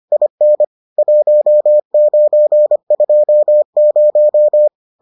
Morse code, heard as a keyed tone, about 25 wpm, 600 Hz. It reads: IN 1920